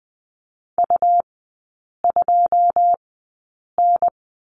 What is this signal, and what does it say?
Morse code, heard as a keyed tone, about 20 wpm, spaced as for 8 wpm, 700 Hz. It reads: U2N
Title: 159 Lesson 26e - Letter g - Three Characters 20wpm